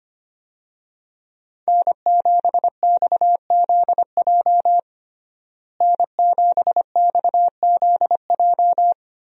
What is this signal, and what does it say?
Morse code, heard as a keyed tone, about 25 wpm, 700 Hz. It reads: N7XZJ N7XZJ
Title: USA Call Signs - 5 Characters - Repeated 3x - 25wpm